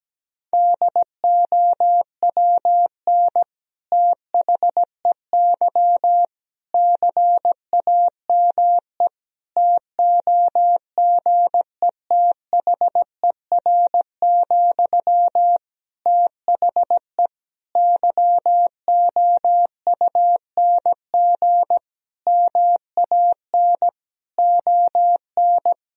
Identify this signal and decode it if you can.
Morse code, heard as a keyed tone, about 17 wpm, 700 Hz. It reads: DOWN THEY CAME TOGETHER, THE YOUNG MAN ON
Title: Sentences from Top 1000 Words - Review Part 2 - 17wpm